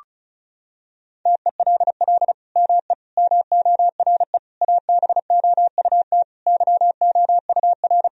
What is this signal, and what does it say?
Morse code, heard as a keyed tone, about 36 wpm, 700 Hz. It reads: TELL ME MORE ABOUT YOUR
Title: Sentences from Top 1000 Words - Review Part 4 - 35wpm